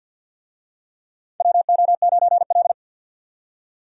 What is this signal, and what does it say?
Morse code, heard as a keyed tone, about 53 wpm, 700 Hz. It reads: WO9L
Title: USA Call Signs - 4 Characters - 50wpm